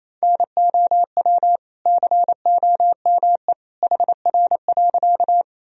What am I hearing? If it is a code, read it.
Morse code, heard as a keyed tone, about 28 wpm, 700 Hz. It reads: NOW COME HR.
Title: Sentences from Top 200 Words and Abbreviations 28wpm